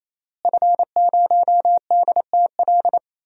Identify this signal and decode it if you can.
Morse code, heard as a keyed tone, about 28 wpm, 700 Hz. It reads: F0DTL
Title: DXCC-100 Call Signs - 5 Characters - Repeated 3x - 28wpm